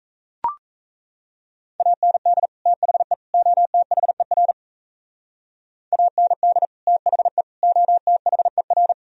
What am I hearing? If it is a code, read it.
Morse code, heard as a keyed tone, about 36 wpm, 700 Hz. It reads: AND THE OTHER AND THE OTHER